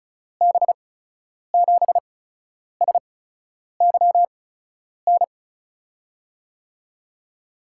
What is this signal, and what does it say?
Morse code, heard as a keyed tone, about 34 wpm, spaced as for 10 wpm, 700 Hz. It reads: B7SYN
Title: USA Call Signs - 6 Characters - 8x Character Spacing - 35wpm